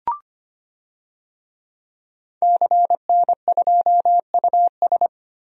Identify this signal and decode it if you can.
Morse code, heard as a keyed tone, about 25 wpm, 700 Hz. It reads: CN2US
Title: DXCC-100 Call Signs - 5 Characters - Repeated 2x - 25wpm